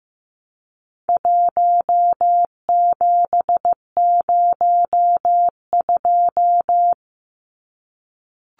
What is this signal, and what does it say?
Morse code, heard as a keyed tone, about 15 wpm, 700 Hz. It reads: 1702
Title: Numbers - 4-Digit 15wpm